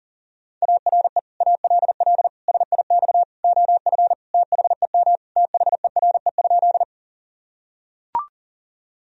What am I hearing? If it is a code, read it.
Morse code, heard as a keyed tone, about 37 wpm, 700 Hz. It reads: ARE ALL SIX OF THEM THERE?
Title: Sentences from Top 1000 Words - Review Part 2 - 40wpm